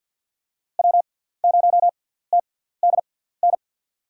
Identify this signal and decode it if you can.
Morse code, heard as a keyed tone, about 53 wpm, spaced as for 17 wpm, 700 Hz. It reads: W0TDN
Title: USA Call Signs - 6x Character Spacing - 50wpm